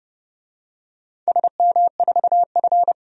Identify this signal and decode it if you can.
Morse code, heard as a keyed tone, about 30 wpm, 700 Hz. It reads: SM4F
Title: DXCC-100 Call Signs - 4 Characters - Repeated 3x - 30wpm